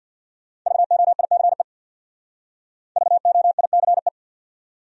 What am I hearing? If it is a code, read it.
Morse code, heard as a keyed tone, about 64 wpm, 700 Hz. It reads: VOICE VOICE